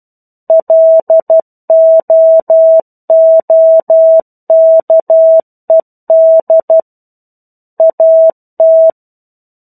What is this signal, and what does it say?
Morse code, heard as a keyed tone, about 12 wpm, 650 Hz. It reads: LOOKED AT
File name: 01-Chapter-01 12 WPM